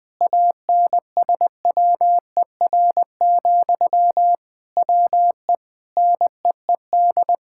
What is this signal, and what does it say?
Morse code, heard as a keyed tone, about 20 wpm, 700 Hz. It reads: ANSWER, WE NEED
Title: Sentences from Top 1600 Words 20wpm